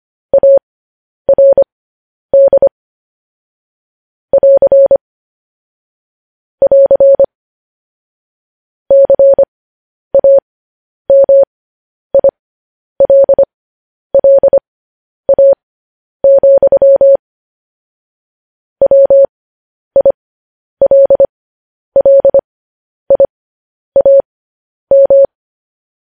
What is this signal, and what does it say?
Morse code, heard as a keyed tone, about 25 wpm, spaced as for 10 wpm, 550 Hz. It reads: ARD <AR> <AR> CAMILLA, WILLIAM